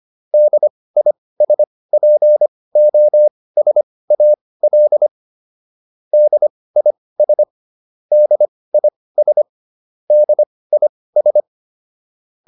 Morse code, heard as a keyed tone, about 25 wpm, spaced as for 18 wpm, 600 Hz. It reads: DISPOSAL DIS DIS DIS
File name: DIS8 18